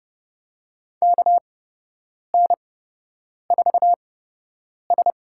Morse code, heard as a keyed tone, about 30 wpm, spaced as for 8 wpm, 700 Hz. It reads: KN4S
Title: USA Call Signs - 4 Characters - 8x Character Spacing - 30wpm